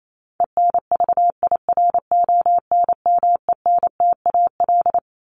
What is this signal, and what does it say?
Morse code, heard as a keyed tone, about 28 wpm, 700 Hz. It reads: ENVIRONMENTAL
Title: Top 1001-1250 Words 28wpm